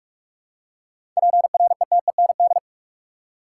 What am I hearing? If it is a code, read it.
Morse code, heard as a keyed tone, about 43 wpm, 700 Hz. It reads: PRETEND
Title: Seven Letter Words - 45wpm